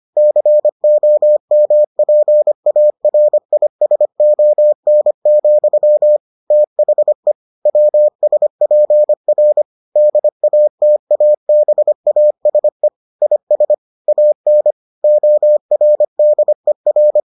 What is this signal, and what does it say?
Morse code, heard as a keyed tone, about 25 wpm, 600 Hz. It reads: COMPARISON, THE WSPR DATABASE IS AN ORDER